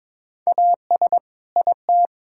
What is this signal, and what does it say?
Morse code, heard as a keyed tone, about 22 wpm, 700 Hz. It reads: AS IT
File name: Build A Phrase - 2-Word Phrases - 22wpm